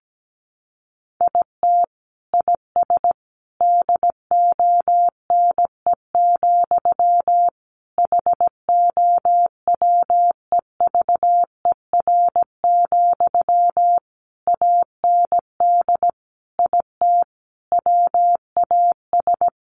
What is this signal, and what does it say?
Morse code, heard as a keyed tone, about 17 wpm, 700 Hz. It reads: IT IS DONE, HOWEVER, AND IT WAS
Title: Sentences from Top 600 Words 17wpm